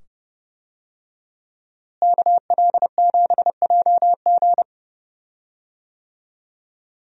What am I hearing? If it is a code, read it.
Morse code, heard as a keyed tone, about 30 wpm, 700 Hz. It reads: KL7JG
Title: DXCC-100 Call Signs - 5 Characters - Repeated 2x - 30wpm